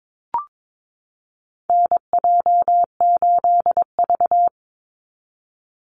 Morse code, heard as a keyed tone, about 22 wpm, 700 Hz. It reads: NJ8V
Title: Call Signs Repeated 2x - 22wpm